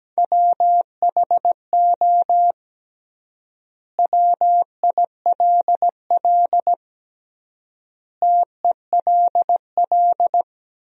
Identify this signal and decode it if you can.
Morse code, heard as a keyed tone, about 17 wpm, 700 Hz. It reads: WHO WILL TELL
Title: 3x Word-Spacing - Sentences from Top 200 Words - 17wpm